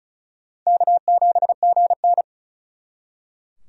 Morse code, heard as a keyed tone, about 46 wpm, 700 Hz. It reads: K7GN